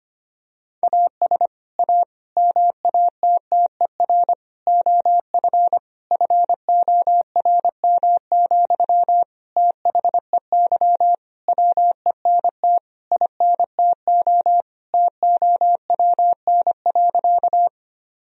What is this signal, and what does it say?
Morse code, heard as a keyed tone, about 25 wpm, 700 Hz. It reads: AS A MATTER OF FORM, THEY WENT INTO TOWN.
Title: Sentences from Top 1000 Words - Review Part 1 - 25wpm